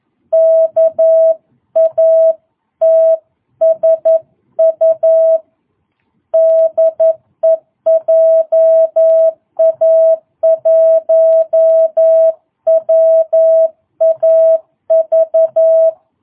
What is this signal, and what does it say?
Morse code, heard as a keyed tone, about 11 wpm, 660 Hz. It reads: KATSU DE JA1WAV